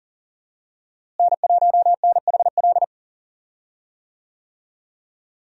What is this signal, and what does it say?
Morse code, heard as a keyed tone, about 40 wpm, 700 Hz. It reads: N1NHL
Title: USA Call Signs - 5 Characters - Repeated 2x - 40wpm